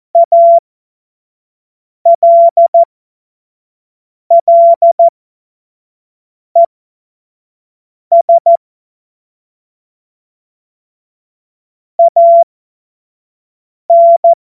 Morse code, heard as a keyed tone, about 14 wpm, spaced as for 5 wpm, 680 Hz. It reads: ALLES AN
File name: Morsekurs / Witze / Witz 20. Juni 2023 25 F70